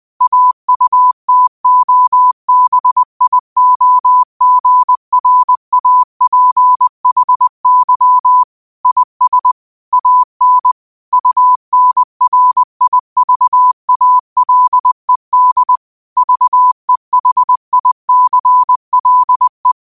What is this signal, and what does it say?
Morse code, heard as a keyed tone, about 20 wpm, 1000 Hz. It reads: AUTOBIOGRAPHY IS AN UNRIVALED VEHICLE
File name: qotdfiles / 20wpm